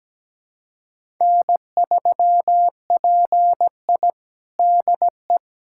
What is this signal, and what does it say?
Morse code, heard as a keyed tone, about 17 wpm, 700 Hz. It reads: N3PI DE
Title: USA Callsign DE Callsign - 4 Characters - 17wpm